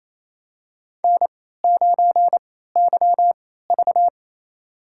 Morse code, heard as a keyed tone, about 28 wpm, spaced as for 16 wpm, 700 Hz. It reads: N9YV
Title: USA Call Signs - 3x Character Spacing - 28wpm